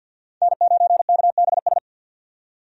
Morse code, heard as a keyed tone, about 50 wpm, 700 Hz. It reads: N9KBS
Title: USA Call Signs - 5 Characters - 50wpm